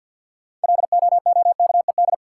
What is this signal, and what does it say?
Morse code, heard as a keyed tone, about 53 wpm, 700 Hz. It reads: LOOKED
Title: Sentences from Top 800 Words - New Word Drill - 50wpm